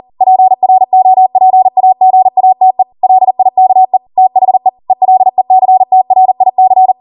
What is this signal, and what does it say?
Morse code, heard as a keyed tone, about 40 wpm, 750 Hz. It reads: PROPAGATE LIKE THE ELECTRIC
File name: Morse / 40WPM